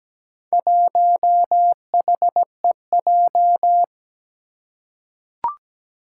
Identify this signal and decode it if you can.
Morse code, heard as a keyed tone, about 17 wpm, 700 Hz. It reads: 1HEJ
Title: Four Challenging Letter-Number Combinations 17wpm